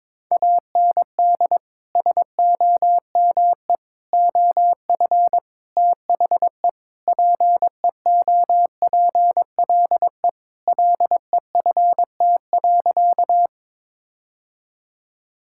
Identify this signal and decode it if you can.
Morse code, heard as a keyed tone, about 22 wpm, 700 Hz. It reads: AND SOME OF THE PEOPLE LEFT.
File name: practice / 22wpm / Sentences with 4-Word Phrases - 22wpm